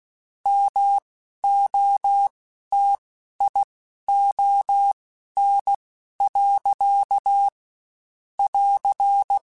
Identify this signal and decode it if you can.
Morse code, heard as a keyed tone, about 16 wpm, spaced as for 11 wpm, 800 Hz. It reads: MOTION. <AR>
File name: Spaced / Chuckles Spaced-mp3s / BrainMouth sp @15